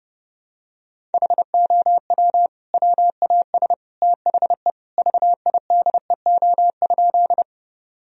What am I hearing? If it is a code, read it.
Morse code, heard as a keyed tone, about 30 wpm, 700 Hz. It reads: HOW WAS THE VIDEO?
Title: Sentences from Top 1000 Words - Review Part 4 - 30wpm